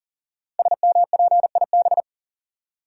Morse code, heard as a keyed tone, about 42 wpm, 700 Hz. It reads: SMPIB
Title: Code Groups - 5 Letters - 40wpm